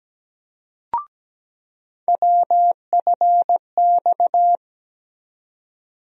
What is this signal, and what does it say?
Morse code, heard as a keyed tone, about 17 wpm, 700 Hz. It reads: WFX
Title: USA Call Sign Postfixes - 3 Letters 17wpm